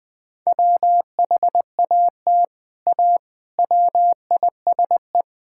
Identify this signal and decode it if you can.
Morse code, heard as a keyed tone, about 20 wpm, 700 Hz. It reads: WHAT A WISE
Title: Sentences from Top 1600 Words 20wpm